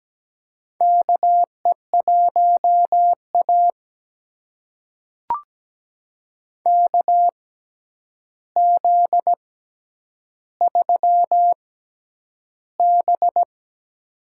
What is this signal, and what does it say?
Morse code, heard as a keyed tone, about 17 wpm, spaced as for 9 wpm, 700 Hz. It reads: KE1A KZ3B
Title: USA Call Signs - 4 Characters - 6x Character Spacing - 17wpm